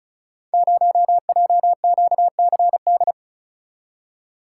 Morse code, heard as a keyed tone, about 36 wpm, 700 Hz. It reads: 0JQCD
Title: Code Groups - 5 Letter:Numbers - 35wpm